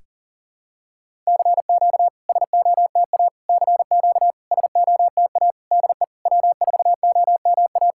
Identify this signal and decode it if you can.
Morse code, heard as a keyed tone, about 43 wpm, 700 Hz. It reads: CQ SOTA CQ SOTA DE W4OMA
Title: SOTA QSO Practice 40wpm